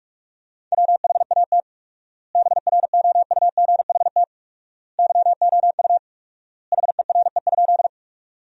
Morse code, heard as a keyed tone, about 46 wpm, 690 Hz. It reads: WHAT BROUGHT YOU HERE?
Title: 4x Word-Spacing - Sentences from Top 500 Words - 45wpm